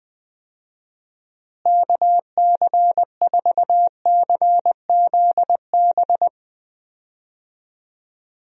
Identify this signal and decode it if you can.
Morse code, heard as a keyed tone, about 20 wpm, 700 Hz. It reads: KC4CZB
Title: USA Call Signs - 6 Characters - Repeated 2x - 20wpm